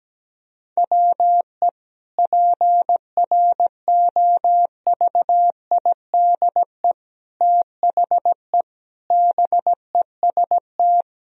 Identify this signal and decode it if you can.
Morse code, heard as a keyed tone, about 17 wpm, 700 Hz. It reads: WE PROVIDE THE BEST
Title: Sentences from Top 1000 Words - Review Part 4 - 17wpm